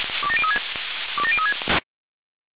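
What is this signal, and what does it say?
Five-tone selective call (ZVEI), heard as quick runs of five tones, tones of about 70 ms. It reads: 27036 27036
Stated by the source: ZVEI five-tone selective call 27084, received on SDR.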